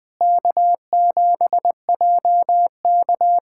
Morse code, heard as a keyed tone, about 20 wpm, 700 Hz. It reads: K7JK
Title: USA Callsign DE Callsign - 4 Characters - 20wpm